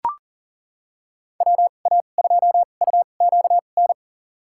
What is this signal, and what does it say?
Morse code, heard as a keyed tone, about 38 wpm, spaced as for 28 wpm, 700 Hz. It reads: WA2UQN
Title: USA Call Signs - 6 Characters - 2x Character Spacing - 40wpm